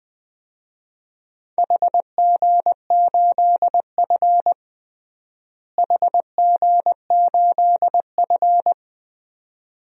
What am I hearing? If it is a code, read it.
Morse code, heard as a keyed tone, about 20 wpm, 700 Hz. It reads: HG8F HG8F
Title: DXCC-100 Call Signs - 4 Characters - Repeated 2x - 20wpm